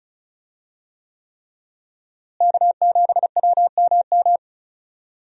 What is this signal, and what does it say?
Morse code, heard as a keyed tone, about 36 wpm, 700 Hz. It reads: K7WMM
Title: USA Call Signs - 5 Characters - Repeated 3x - 35wpm